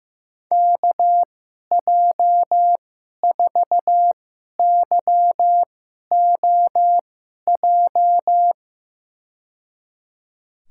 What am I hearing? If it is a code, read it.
Morse code, heard as a keyed tone, about 15 wpm, spaced as for 11 wpm, 700 Hz. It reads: KJ4YOJ
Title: USA Call Signs - 6 Characters - 2x Character Spacing - 15wpm